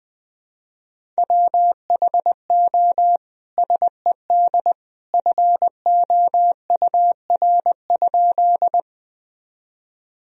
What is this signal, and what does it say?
Morse code, heard as a keyed tone, about 20 wpm, 700 Hz. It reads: WHO SED FOUR?
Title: Sentences from Top 300 Words and Abbreviations 20wpm